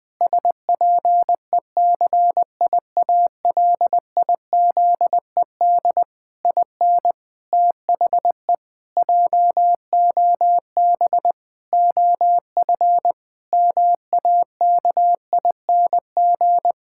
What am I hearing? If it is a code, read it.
Morse code, heard as a keyed tone, about 21 wpm, 700 Hz. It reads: SPECIALIZED IN THE JOB OF MAKING
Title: Pangram Challenge v2 - 20wpm